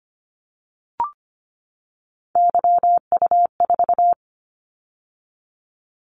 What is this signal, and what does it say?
Morse code, heard as a keyed tone, about 25 wpm, 700 Hz. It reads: YU4
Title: Callsign Prefixes for Top 25 DXCC - 25wpm